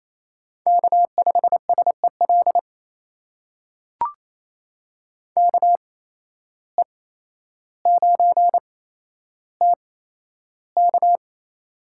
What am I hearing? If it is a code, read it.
Morse code, heard as a keyed tone, about 28 wpm, spaced as for 12 wpm, 700 Hz. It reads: K5SEL KE9TK
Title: USA Call Signs - 5 Characters - 8x Character Spacing - 28wpm